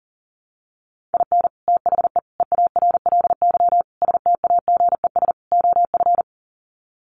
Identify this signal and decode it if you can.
Morse code, heard as a keyed tone, about 40 wpm, 700 Hz. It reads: IN THE EARLY STAGES OF
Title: Sets of 5 Words Encore 40wpm